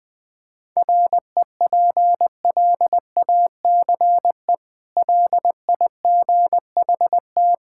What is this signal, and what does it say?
Morse code, heard as a keyed tone, about 20 wpm, 700 Hz. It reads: REPLACE LIGHT
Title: Easy Sentences with Hints - 20wpm